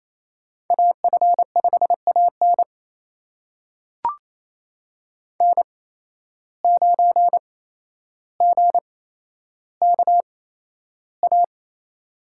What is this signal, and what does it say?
Morse code, heard as a keyed tone, about 28 wpm, spaced as for 12 wpm, 700 Hz. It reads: AF5AN N9GKA
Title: USA Call Signs - 5 Characters - 8x Character Spacing - 28wpm